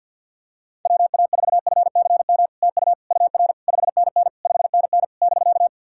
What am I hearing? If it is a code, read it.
Morse code, heard as a keyed tone, about 51 wpm, 690 Hz. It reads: WA4FCM TU UR 5NN 5NN <BK>
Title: SOTA QSO Practice 50wpm